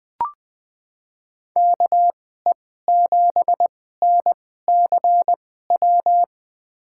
Morse code, heard as a keyed tone, about 21 wpm, spaced as for 15 wpm, 700 Hz. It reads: KE7NCW
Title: USA Call Signs - 6 Characters - 2x Character Spacing - 20wpm